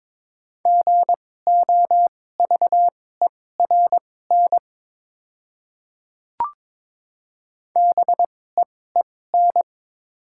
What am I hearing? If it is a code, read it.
Morse code, heard as a keyed tone, about 22 wpm, spaced as for 16 wpm, 700 Hz. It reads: GOVERN BEEN
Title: 2x Character Spacing - Top 500 Words - 22wpm